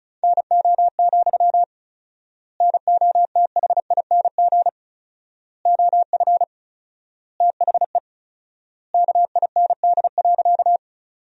Morse code, heard as a keyed tone, about 36 wpm, 700 Hz. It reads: NO, NOTHING OF THE KIND.
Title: 4x Word-Spacing - Sentences from Top 500 Words - 35wpm